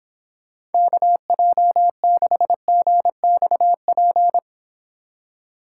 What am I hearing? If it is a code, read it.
Morse code, heard as a keyed tone, about 29 wpm, 700 Hz. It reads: KJ6GXP